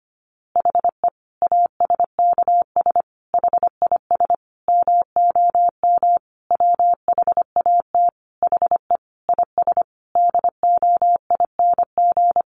Morse code, heard as a keyed tone, about 25 wpm, 700 Hz. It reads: HE ASKS HIS MOM WHAT HE IS DOING
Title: Easy Phrases 25wpm